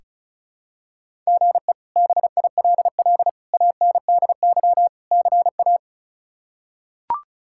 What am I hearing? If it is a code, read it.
Morse code, heard as a keyed tone, about 36 wpm, 700 Hz. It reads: GE BILL ANDY CA
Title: K1USN SST QSO Practice - 35wpm